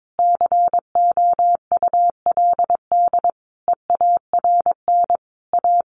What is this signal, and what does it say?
Morse code, heard as a keyed tone, about 22 wpm, 700 Hz. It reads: COULD EARN A
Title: Sentences from Top 1500 Words 22wpm